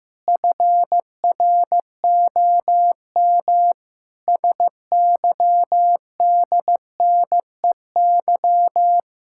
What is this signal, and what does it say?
Morse code, heard as a keyed tone, about 16 wpm, 700 Hz. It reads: FROM SYDNEY
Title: Pangram Challenge v2 - 15wpm